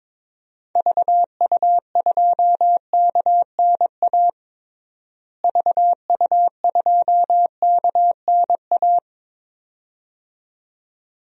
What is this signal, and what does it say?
Morse code, heard as a keyed tone, about 21 wpm, 700 Hz. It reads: VU2KNA VU2KNA
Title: DXCC-100 Call Signs - 6 Characters - Repeated 2x - 22wpm